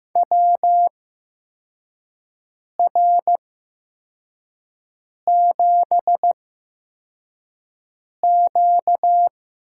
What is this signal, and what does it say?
Morse code, heard as a keyed tone, about 15 wpm, spaced as for 4 wpm, 700 Hz. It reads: WR7Q
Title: USA Call Signs - 4 Characters - 8x Character Spacing - 15wpm